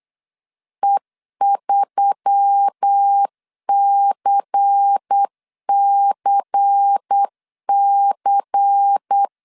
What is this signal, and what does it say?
Morse code, heard as a keyed tone, about 8 wpm, 800 Hz. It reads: E3CCC
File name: CW / DET / 1 6 1500C S B 8wpm 5142